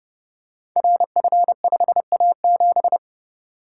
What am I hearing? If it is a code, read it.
Morse code, heard as a keyed tone, about 30 wpm, 700 Hz. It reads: RF5A7
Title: Code Groups - 5 Letter:Numbers - 30wpm